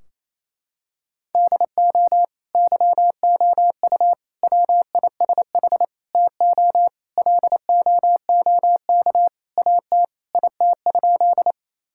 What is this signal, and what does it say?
Morse code, heard as a keyed tone, about 28 wpm, 700 Hz. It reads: DO YOU WISH TO LOOK AT IT?
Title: Sentences from Top 1600 Words 28wpm